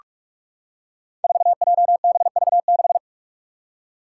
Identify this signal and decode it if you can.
Morse code, heard as a keyed tone, about 43 wpm, 700 Hz. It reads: 4JBV6
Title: Five Challenging Letter-Number Combinations 45wpm